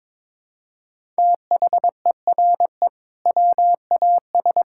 Morse code, heard as a keyed tone, about 22 wpm, 700 Hz. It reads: THERE WAS
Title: Sentences from Top 1000 Words - Review Part 6 - 22wpm